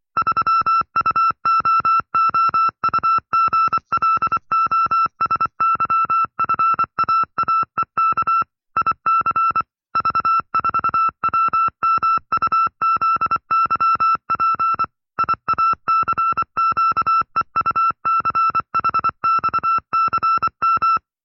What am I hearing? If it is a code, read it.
Morse code, heard as a keyed tone, about 24 wpm, 1330 Hz. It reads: 3UOOUGLOSYFAAEK IC V4WMUZYP IACQEUCHXCM
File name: FreeFiles / CW-practice / 20WPM-side2